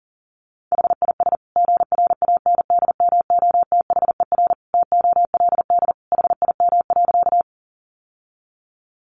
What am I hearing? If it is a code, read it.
Morse code, heard as a keyed tone, about 40 wpm, 700 Hz. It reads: HIS GRANDMOTHER TOLD HIM.
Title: Sentences from Top 1200 Words 40wpm